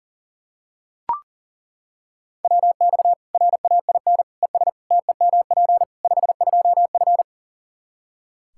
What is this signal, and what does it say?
Morse code, heard as a keyed tone, about 40 wpm, 700 Hz. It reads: WX RAIN ES TEMP 52F
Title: Practice QSO - Exchange 2 - Info - 40wpm